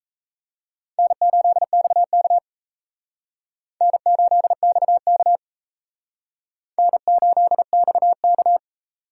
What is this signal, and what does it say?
Morse code, heard as a keyed tone, about 36 wpm, 700 Hz. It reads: N8XK N8XK N8XK